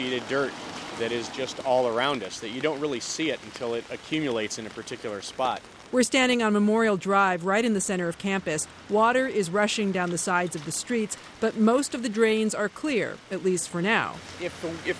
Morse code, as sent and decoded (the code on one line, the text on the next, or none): none